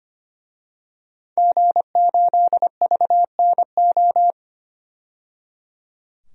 Morse code, heard as a keyed tone, about 25 wpm, 700 Hz. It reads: G8VNO
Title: DXCC-100 Call Signs - 5 Characters - 25wpm